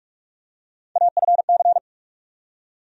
Morse code, heard as a keyed tone, about 46 wpm, 700 Hz. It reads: AFC
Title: USA Call Sign Postfixes - 3 Letters 45wpm